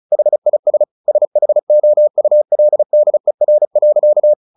Morse code, heard as a keyed tone, about 34 wpm, 600 Hz. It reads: HIS SHOULDER.
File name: the old man and the sea / the old man and the sea 1 35wpm